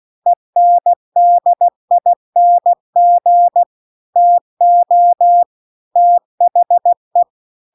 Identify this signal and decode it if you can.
Morse code, heard as a keyed tone, about 16 wpm, 700 Hz. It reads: ENDING TO THE